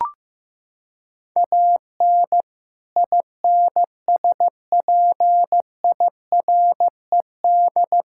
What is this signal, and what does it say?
Morse code, heard as a keyed tone, about 16 wpm, 700 Hz. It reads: AN INSPIRED
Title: Pangram Challenge v2 - 15wpm